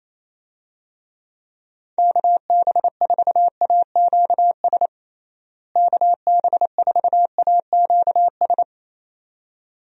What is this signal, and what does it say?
Morse code, heard as a keyed tone, about 28 wpm, 700 Hz. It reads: KB4AQS KB4AQS
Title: USA Call Signs - 6 Characters - Repeated 2x - 28wpm